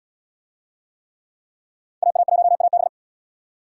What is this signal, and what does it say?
Morse code, heard as a keyed tone, about 77 wpm, 700 Hz. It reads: AA1AB